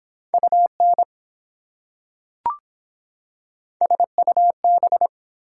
Morse code, heard as a keyed tone, about 24 wpm, 700 Hz. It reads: UN SUB